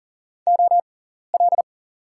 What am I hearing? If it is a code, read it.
Morse code, heard as a keyed tone, about 40 wpm, spaced as for 14 wpm, 700 Hz. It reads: OL
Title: 6x Character Spacing - Top 300 Words - 40wpm